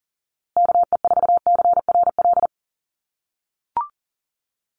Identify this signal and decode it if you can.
Morse code, heard as a keyed tone, about 38 wpm, 700 Hz. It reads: KE4CRL
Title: USA Call Signs Repeated 3x 40wpm